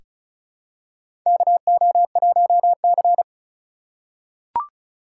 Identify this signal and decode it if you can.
Morse code, heard as a keyed tone, about 36 wpm, 700 Hz. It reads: KO1C
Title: USA Call Signs - 4x Character Spacing - 35wpm